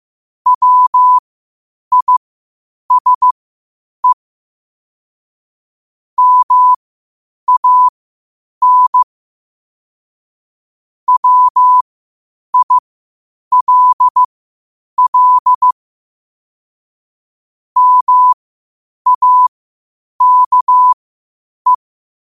Morse code, heard as a keyed tone, about 15 wpm, spaced as for 8 wpm, 1000 Hz. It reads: WISE MAN WILL MAKE
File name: qotdfiles / 8wpm